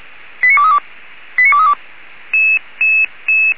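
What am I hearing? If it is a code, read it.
Five-tone selective call (ZVEI), heard as quick runs of five tones, tones of about 70 ms. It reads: 89232 89232
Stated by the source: ZVEI five-tone call received by SDR.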